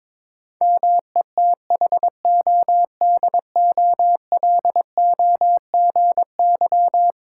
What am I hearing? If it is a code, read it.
Morse code, heard as a keyed tone, about 22 wpm, 700 Hz. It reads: METHODOLOGY
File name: practice / 22wpm / Eleven Letter Words - 22wpm